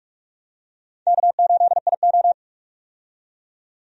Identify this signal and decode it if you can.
Morse code, heard as a keyed tone, about 46 wpm, 700 Hz. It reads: K8IO
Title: USA Call Signs - 4 Characters - 45wpm